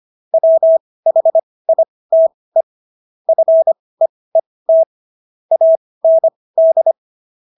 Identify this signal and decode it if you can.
Morse code, heard as a keyed tone, about 25 wpm, spaced as for 18 wpm, 650 Hz. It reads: WHITE FEET AND